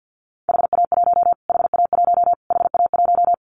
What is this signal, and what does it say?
Morse code, heard as a keyed tone, about 50 wpm, 700 Hz. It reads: HA1 HA1 HA1
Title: Callsign Prefixes for Top 25 DXCC - Repeated 3x - 50wpm